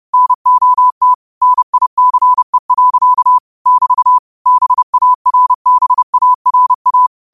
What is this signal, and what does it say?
Morse code, heard as a keyed tone, about 30 wpm, 1000 Hz. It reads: NOT NICE. <BT> BARBARA
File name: qotdfiles / 30wpm